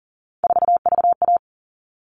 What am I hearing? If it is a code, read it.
Morse code, heard as a keyed tone, about 40 wpm, 700 Hz. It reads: 4VA
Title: All Cut Numbers - 3-Digit 40wpm